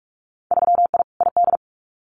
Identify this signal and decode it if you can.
Morse code, heard as a keyed tone, about 46 wpm, 700 Hz. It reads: 3I ID
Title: Winter Field Day Exchanges Encore 45wpm